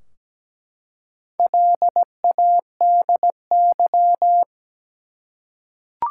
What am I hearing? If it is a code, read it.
Morse code, heard as a keyed tone, about 17 wpm, 700 Hz. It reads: LADY
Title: Four Letter Words - 17wpm